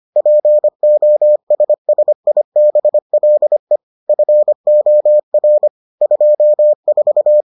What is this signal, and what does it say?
Morse code, heard as a keyed tone, about 25 wpm, 600 Hz. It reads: POSSIBLE FOR 24